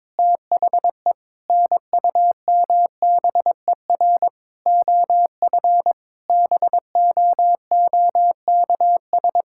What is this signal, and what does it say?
Morse code, heard as a keyed tone, about 22 wpm, 700 Hz. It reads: THE NUMBER OF BOOKS
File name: practice / 22wpm / Sentences with 8-Word Phrases - 22wpm